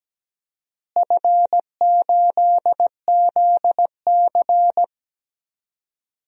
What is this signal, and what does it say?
Morse code, heard as a keyed tone, about 17 wpm, 700 Hz. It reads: F8ZC
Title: DXCC-100 Call Signs - 4 Characters - Repeated 3x - 17wpm